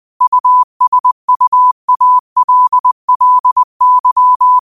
Morse code, heard as a keyed tone, about 20 wpm, 1000 Hz. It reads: USUALLY
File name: qotdfiles / 20wpm